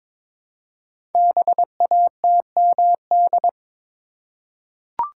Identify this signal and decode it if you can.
Morse code, heard as a keyed tone, about 22 wpm, 700 Hz. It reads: BATMD
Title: Code Groups - 5 Letters - 22wpm